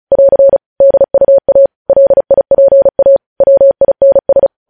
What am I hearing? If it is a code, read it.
Morse code, heard as a keyed tone, about 36 wpm, 550 Hz. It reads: <AR> DUA LIPA WINS
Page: News headlines in Morse code at 35 wpm